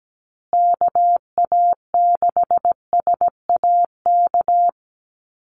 Morse code, heard as a keyed tone, about 17 wpm, 700 Hz. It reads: KA6SAK
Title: USA Call Signs Repeated 2x 17wpm